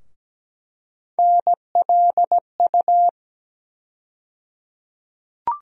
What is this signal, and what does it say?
Morse code, heard as a keyed tone, about 17 wpm, 700 Hz. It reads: NLU
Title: USA Call Sign Postfixes - 3 Letters 17wpm